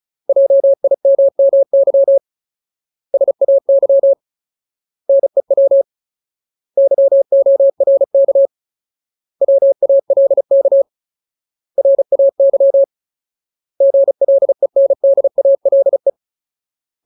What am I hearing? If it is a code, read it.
Morse code, heard as a keyed tone, about 34 wpm, 550 Hz. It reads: JIMMY SAY NEW YORK WALK RAY GLENDALE
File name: Ultimate Word List / 35WPM Ultimate